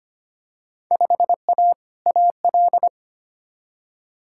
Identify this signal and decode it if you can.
Morse code, heard as a keyed tone, about 25 wpm, 700 Hz. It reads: 5A AL
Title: Field Day Exchanges Encore - 25wpm